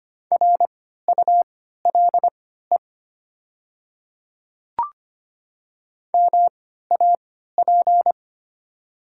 Morse code, heard as a keyed tone, about 25 wpm, spaced as for 14 wpm, 700 Hz. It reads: RULE MAP
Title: 3x Character Spacing - Top 500 Words - 25wpm